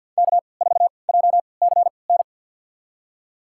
Morse code, heard as a keyed tone, about 52 wpm, spaced as for 29 wpm, 700 Hz. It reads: K4JCN
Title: USA Call Signs - 5 Characters - 3x Character Spacing - 50wpm